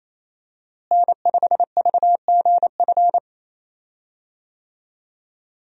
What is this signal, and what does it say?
Morse code, heard as a keyed tone, about 27 wpm, 700 Hz. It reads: N5VGF
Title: USA Call Signs - 5 Characters - Repeated 3x - 28wpm